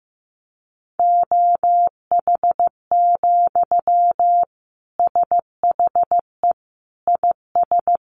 Morse code, heard as a keyed tone, about 15 wpm, 700 Hz. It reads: OH, SHE IS
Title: Sentences from Top 1100 Words 15wpm